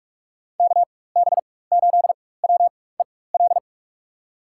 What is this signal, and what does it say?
Morse code, heard as a keyed tone, about 47 wpm, spaced as for 21 wpm, 700 Hz. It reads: KB8WEL
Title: USA Call Signs - 6 Characters - 4x Character Spacing - 45wpm